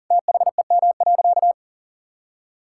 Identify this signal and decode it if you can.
Morse code, heard as a keyed tone, about 40 wpm, 700 Hz. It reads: THEM.
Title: Sentences from Top 1000 Words - Review Part 5 - 40wpm